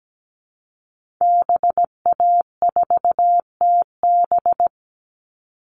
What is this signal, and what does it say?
Morse code, heard as a keyed tone, about 17 wpm, 700 Hz. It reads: BA4TB
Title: Call Signs Repeated 2x - 17wpm